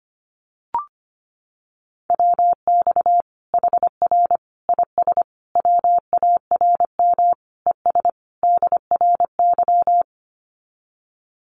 Morse code, heard as a keyed tone, about 25 wpm, 700 Hz. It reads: WX HR IS WARM ES DRY
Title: Common QSO Phrases 25wpm